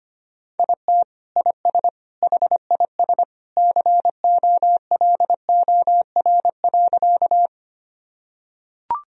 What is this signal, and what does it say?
Morse code, heard as a keyed tone, about 25 wpm, 700 Hz. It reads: IT IS HIS COLOR.
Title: Sentences from Top 1000 Words - Review Part 2 - 25wpm